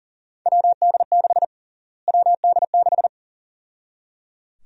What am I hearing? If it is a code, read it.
Morse code, heard as a keyed tone, about 37 wpm, 700 Hz. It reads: WD6 WD6
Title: USA Call Sign Prefixes - Repeated 2x - 40wpm